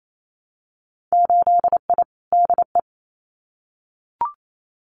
Morse code, heard as a keyed tone, about 28 wpm, 700 Hz. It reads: 8I DE
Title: Winter Field Day Exchanges Encore 28wpm